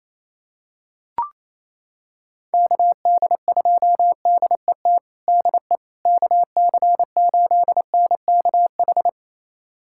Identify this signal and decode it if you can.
Morse code, heard as a keyed tone, about 28 wpm, 700 Hz. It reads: KD2DET DE KC8NKH
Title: USA Callsign DE Callsign - 6 Characters - 28wpm